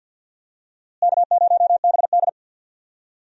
Morse code, heard as a keyed tone, about 46 wpm, 700 Hz. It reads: K0BD
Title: USA Call Signs - 4 Characters - Repeated 2x - 50wpm